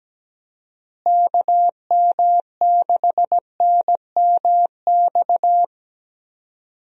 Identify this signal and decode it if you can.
Morse code, heard as a keyed tone, about 17 wpm, 700 Hz. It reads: KM6NMX
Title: USA Call Signs - 6 Characters - Repeated 2x - 17wpm